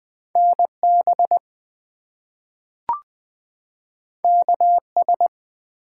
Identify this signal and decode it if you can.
Morse code, heard as a keyed tone, about 21 wpm, 700 Hz. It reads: NB KS
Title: US States and Canadian Provinces 20wpm